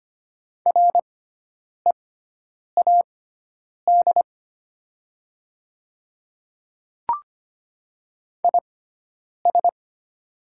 Morse code, heard as a keyed tone, about 25 wpm, spaced as for 9 wpm, 700 Hz. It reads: READ IS
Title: 6x Character Spacing - Top 300 Words - 25wpm